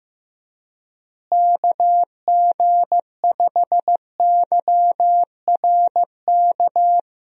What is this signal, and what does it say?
Morse code, heard as a keyed tone, about 15 wpm, 700 Hz. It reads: KG5YRK
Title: USA Call Signs - 6 Characters - Repeated 3x - 15wpm